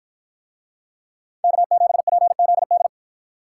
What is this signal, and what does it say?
Morse code, heard as a keyed tone, about 50 wpm, 700 Hz. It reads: K7PZD